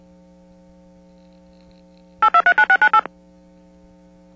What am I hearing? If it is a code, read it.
DTMF, heard as telephone touch tones, tones of about 80 ms. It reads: 8239698
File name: TurtleCreekEWSD